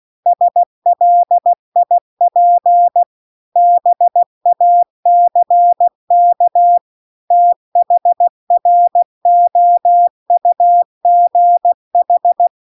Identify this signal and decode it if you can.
Morse code, heard as a keyed tone, about 16 wpm, 700 Hz. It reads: SLIP BACK THROUGH
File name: CW-The-Big-Twoheaded-River-2-16WPM